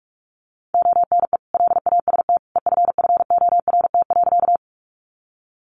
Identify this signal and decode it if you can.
Morse code, heard as a keyed tone, about 44 wpm, 700 Hz. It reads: ONE LAST EFFORT.
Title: Sentences from Top 1200 Words 45wpm